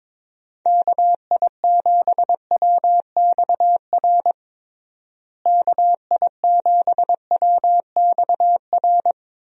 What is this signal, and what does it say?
Morse code, heard as a keyed tone, about 21 wpm, 700 Hz. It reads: KI7WXR KI7WXR
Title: USA Call Signs - 6 Characters - Repeated 3x - 22wpm